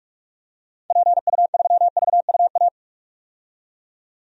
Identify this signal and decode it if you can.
Morse code, heard as a keyed tone, about 42 wpm, 700 Hz. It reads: PU3VUA
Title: DXCC-100 Call Signs - 6 Characters - 45wpm